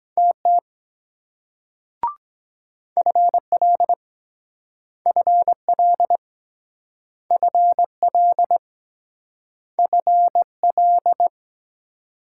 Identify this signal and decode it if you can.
Morse code, heard as a keyed tone, about 23 wpm, 700 Hz. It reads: TT FL FL FL FL